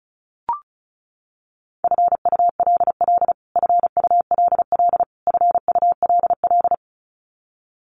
Morse code, heard as a keyed tone, about 34 wpm, 700 Hz. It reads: FULL FULL FULL
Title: Top 500 Words Repeated 35wpm